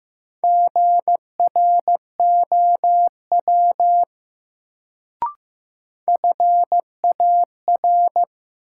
Morse code, heard as a keyed tone, about 15 wpm, 700 Hz. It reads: GROW FAR
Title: Sentences from Top 300 Words - New Word Drill - 15wpm